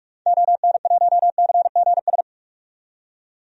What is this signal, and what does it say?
Morse code, heard as a keyed tone, about 42 wpm, 700 Hz. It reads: ON1CGS
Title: DXCC-100 Call Signs - 6 Characters - 45wpm